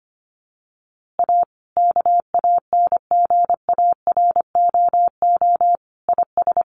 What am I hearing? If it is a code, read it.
Morse code, heard as a keyed tone, about 25 wpm, 700 Hz. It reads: A KANGAROO IS
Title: Easy Phrases 25wpm